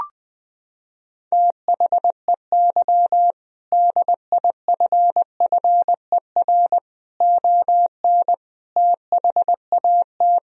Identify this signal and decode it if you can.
Morse code, heard as a keyed tone, about 21 wpm, 700 Hz. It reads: THEY DIFFER ON THAT
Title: Sentences from Top 1000 Words - Review Part 1 - 20wpm